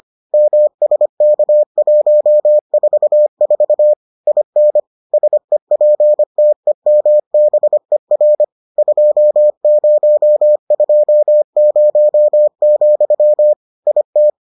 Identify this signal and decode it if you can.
Morse code, heard as a keyed tone, about 25 wpm, 600 Hz. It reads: MSK144 IN SEPTEMBER 2020, IT